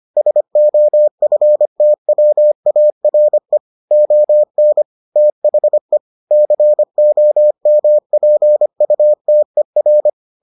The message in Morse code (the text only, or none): SOFTWARE ON THE COMPUTER